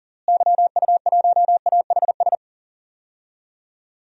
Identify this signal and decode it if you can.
Morse code, heard as a keyed tone, about 37 wpm, 700 Hz. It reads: YU1AHS
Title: DXCC-100 Call Signs - 6 Characters - Repeated 3x - 40wpm